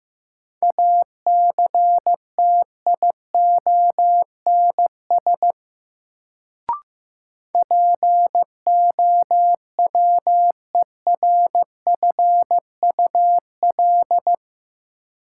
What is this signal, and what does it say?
Morse code, heard as a keyed tone, about 15 wpm, 700 Hz. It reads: ACTIONS POWERFUL
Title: Sentences from Top 1000 Words - Review Part 6 - New Word Drill - 15wpm